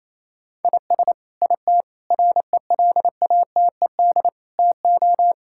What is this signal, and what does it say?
Morse code, heard as a keyed tone, about 28 wpm, 700 Hz. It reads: IS IT RELATED TO
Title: Sentences from Top 1000 Words - Review Part 5 - 28wpm